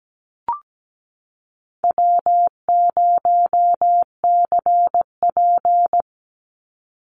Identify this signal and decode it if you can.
Morse code, heard as a keyed tone, about 17 wpm, 700 Hz. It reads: W0CP
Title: NA SOTA Activator Callsigns 17wpm